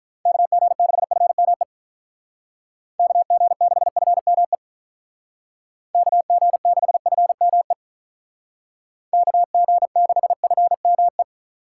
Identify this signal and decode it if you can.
Morse code, heard as a keyed tone, about 43 wpm, 700 Hz. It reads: KG6FME KG6FME KG6FME KG6FME
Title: Speed Racing - USA 2x3 Call Signs 35wpm